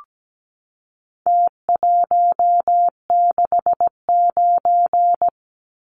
Morse code, heard as a keyed tone, about 17 wpm, 700 Hz. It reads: T169
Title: Numbers - Contest Style - 4-Digit 17wpm